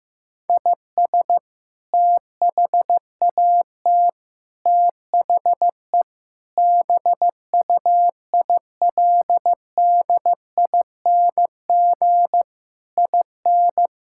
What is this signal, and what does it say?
Morse code, heard as a keyed tone, about 15 wpm, 700 Hz. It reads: IS THAT THE BUILDING IN
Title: Sentences from Top 1000 Words - Review Part 5 - 15wpm